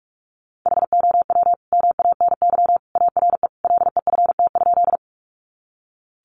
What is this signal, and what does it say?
Morse code, heard as a keyed tone, about 44 wpm, 700 Hz. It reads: HOW MANY ARE LEFT?
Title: Sentences from Top 300 Words 45wpm